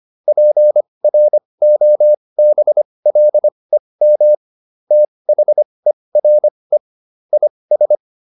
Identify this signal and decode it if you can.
Morse code, heard as a keyed tone, about 25 wpm, spaced as for 20 wpm, 600 Hz. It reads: PROBLEM THERE IS